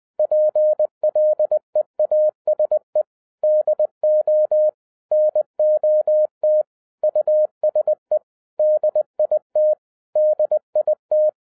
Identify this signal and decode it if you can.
Morse code, heard as a keyed tone, about 20 wpm, 610 Hz. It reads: PLEASE DO NOT USE DIT DIT